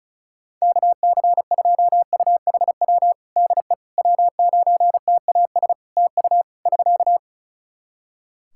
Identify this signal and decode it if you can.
Morse code, heard as a keyed tone, about 36 wpm, 700 Hz. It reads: KC2UHW DE W9TAS TU <SK>
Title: Practice QSO - Exchange 4 - Ending - 35wpm